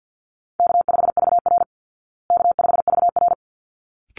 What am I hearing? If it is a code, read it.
Morse code, heard as a keyed tone, about 50 wpm, 700 Hz. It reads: K5VR K5VR
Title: Call Signs Repeated 3x - 50wpm